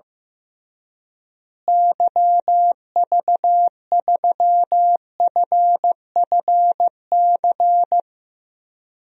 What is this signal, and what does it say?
Morse code, heard as a keyed tone, about 15 wpm, 700 Hz. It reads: YV3FFC
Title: DXCC-100 Call Signs - 6 Characters - Repeated 3x - 15wpm